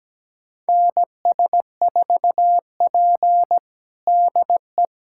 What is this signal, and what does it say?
Morse code, heard as a keyed tone, about 17 wpm, 700 Hz. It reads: NS4P DE
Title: USA Callsign DE Callsign - 4 Characters - 17wpm